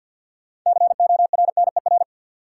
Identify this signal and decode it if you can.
Morse code, heard as a keyed tone, about 47 wpm, 700 Hz. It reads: CORNER
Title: Sentences from Top 1600 Words - New Word Drill - 50wpm